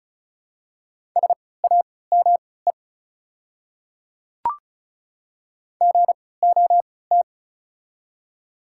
Morse code, heard as a keyed tone, about 36 wpm, spaced as for 20 wpm, 700 Hz. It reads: SAME GOT
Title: 3x Character Spacing - Top 300 Words - 35wpm